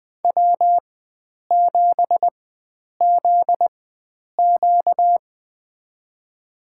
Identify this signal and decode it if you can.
Morse code, heard as a keyed tone, about 21 wpm, spaced as for 9 wpm, 700 Hz. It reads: W7ZQ
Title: USA Call Signs - 4x Character Spacing - 20wpm